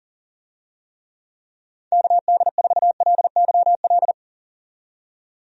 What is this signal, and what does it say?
Morse code, heard as a keyed tone, about 40 wpm, 700 Hz. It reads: KD4LYL
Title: USA Call Signs - 6 Characters - 40wpm